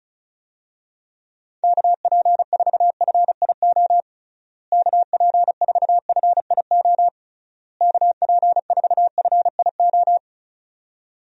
Morse code, heard as a keyed tone, about 36 wpm, 700 Hz. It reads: KP4FIO KP4FIO KP4FIO
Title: USA Call Signs - 6 Characters - Repeated 3x - 35wpm